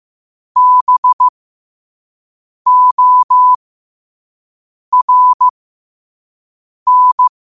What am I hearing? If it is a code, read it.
Morse code, heard as a keyed tone, about 15 wpm, spaced as for 5 wpm, 1000 Hz. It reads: BORN
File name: qotdfiles / 5wpm